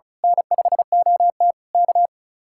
Morse code, heard as a keyed tone, about 33 wpm, 700 Hz. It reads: N5OT K
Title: Speed-Racing Common QSO Phrases - 35wpm